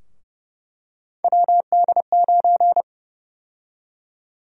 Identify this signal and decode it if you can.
Morse code, heard as a keyed tone, about 32 wpm, 700 Hz. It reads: WD9
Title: USA Call Sign Prefixes - 30wpm